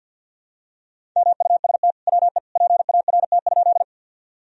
Morse code, heard as a keyed tone, about 46 wpm, 700 Hz. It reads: MUST WE PART?
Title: Sentences from Top 1000 Words - Review Part 1 - 50wpm